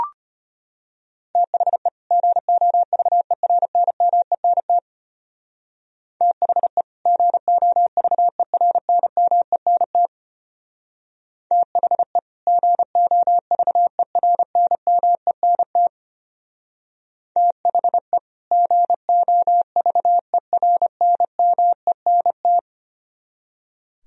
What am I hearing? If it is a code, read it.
Morse code, heard as a keyed tone, about 31 wpm, 700 Hz. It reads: THE GOVERNMENT THE GOVERNMENT THE GOVERNMENT THE GOVERNMENT
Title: Speed-Racing - Sets of 2 Words 25wpm